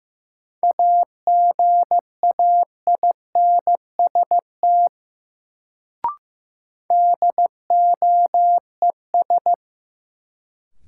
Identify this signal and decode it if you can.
Morse code, heard as a keyed tone, about 15 wpm, 700 Hz. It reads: AGAINST DOES
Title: Sets of 2 Words - New Word Drill - 15wpm